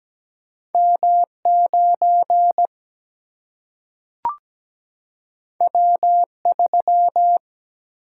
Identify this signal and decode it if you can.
Morse code, heard as a keyed tone, about 17 wpm, 700 Hz. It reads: M9 W3
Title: Single-Letter Single-Number 17wpm